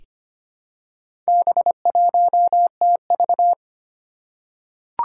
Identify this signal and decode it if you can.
Morse code, heard as a keyed tone, about 25 wpm, 700 Hz. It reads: B1TV
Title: Four Challenging Letter-Number Combinations 25wpm